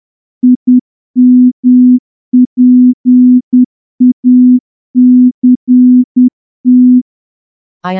Morse code, heard as a keyed tone, about 10 wpm, 250 Hz. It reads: IMPACT
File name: morse-training-10WPM